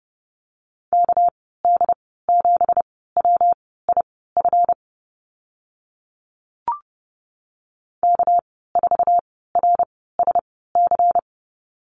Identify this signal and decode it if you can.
Morse code, heard as a keyed tone, about 30 wpm, spaced as for 17 wpm, 700 Hz. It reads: KD7WIF K4RSC
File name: Lesson 37g - Letter x - Call Signs 30wpm